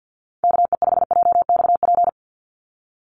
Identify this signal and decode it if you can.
Morse code, heard as a keyed tone, about 50 wpm, 700 Hz. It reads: KE5JXP
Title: USA Call Signs 50wpm